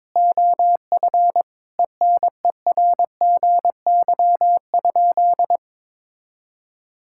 Morse code, heard as a keyed tone, about 22 wpm, 700 Hz. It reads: OF ENERGY?
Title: Sentences from Top 1000 Words - Review Part 6 - 22wpm